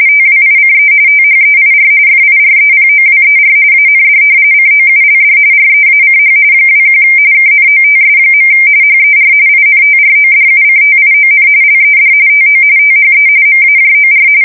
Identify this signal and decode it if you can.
RTTY, about 45.45 baud. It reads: TWICE THEIR FIREPOWER AND COULD REACH 21 KNOTS. HER INTRODUCTION LED TO THE MAJOR
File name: fsk170 / battleshipnj